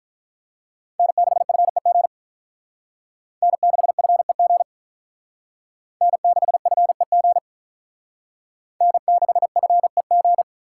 Morse code, heard as a keyed tone, about 45 wpm, 700 Hz. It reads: N6FEG N6FEG N6FEG N6FEG